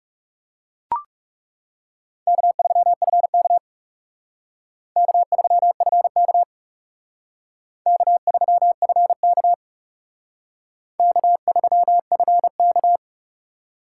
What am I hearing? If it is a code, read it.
Morse code, heard as a keyed tone, about 37 wpm, 700 Hz. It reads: K3FK K3FK K3FK K3FK
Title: Speed Racing - USA 1x2 Call Signs 30wpm